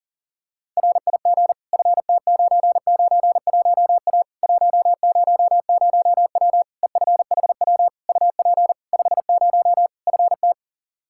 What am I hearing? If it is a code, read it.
Morse code, heard as a keyed tone, about 40 wpm, 700 Hz. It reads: RIG FT991A 100W EFHW UP 50 FT
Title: Practice QSO - Exchange 2 - Info - 40wpm